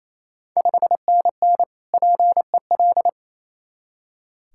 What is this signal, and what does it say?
Morse code, heard as a keyed tone, about 28 wpm, 700 Hz. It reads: 5NN PEL
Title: Ontario QSO Party Exchange 28wpm